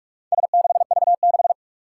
Morse code, heard as a keyed tone, about 47 wpm, 700 Hz. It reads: S6V6
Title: Four Challenging Letter-Number Combinations 45wpm